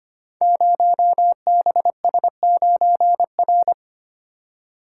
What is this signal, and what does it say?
Morse code, heard as a keyed tone, about 25 wpm, 700 Hz. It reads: 0BS9R
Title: Code Groups - 5 Letter:Numbers - 25wpm